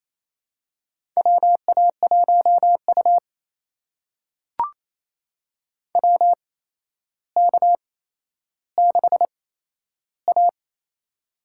Morse code, heard as a keyed tone, about 28 wpm, spaced as for 12 wpm, 700 Hz. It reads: WA1U WK6A
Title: USA Call Signs - 4 Characters - 8x Character Spacing - 28wpm